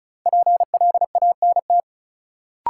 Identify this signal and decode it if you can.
Morse code, heard as a keyed tone, about 36 wpm, 700 Hz. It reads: PLANT
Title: Sentences from Top 1000 Words - Review Part 1 - New Word Drill - 35wpm